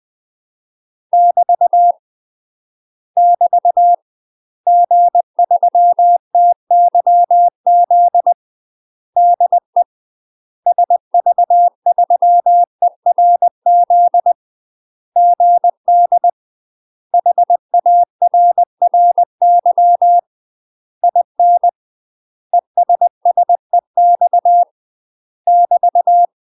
Morse code, heard as a keyed tone, about 20 wpm, 700 Hz. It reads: <BT> <BT> G3TYZ DE SV3ERZ GD HARRY IN ESSEX <BT>
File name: UK MASTER QSOs TRAINING / 20WPM / UK MASTER QSOs 0044